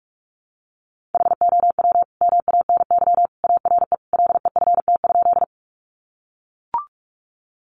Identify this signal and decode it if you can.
Morse code, heard as a keyed tone, about 44 wpm, 700 Hz. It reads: HOW MANY ARE LEFT?
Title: Sentences from Top 300 Words 45wpm